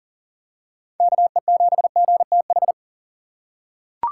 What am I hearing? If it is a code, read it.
Morse code, heard as a keyed tone, about 40 wpm, 700 Hz. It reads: KE7GTH
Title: USA Call Signs - 6 Characters - 2x Character Spacing - 40wpm